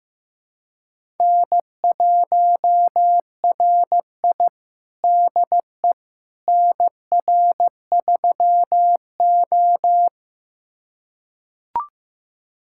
Morse code, heard as a keyed tone, about 15 wpm, 700 Hz. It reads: N1RI DE NR3O
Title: USA Callsign DE Callsign - 4 Characters - 15wpm